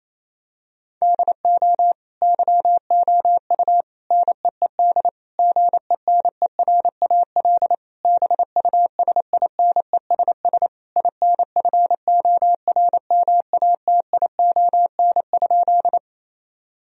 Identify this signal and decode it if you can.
Morse code, heard as a keyed tone, about 28 wpm, 700 Hz. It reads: DO YOU NEED GENERAL BUSINESS INFORMATION?
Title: Sentences from Top 1000 Words - Review Part 3 - 28wpm